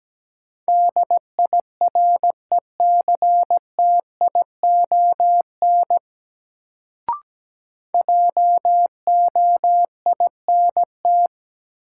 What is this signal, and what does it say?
Morse code, heard as a keyed tone, about 17 wpm, 700 Hz. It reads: DIRECTION JOINT
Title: Sentences from Top 1300 Words - New Word Drill Alt - 17wpm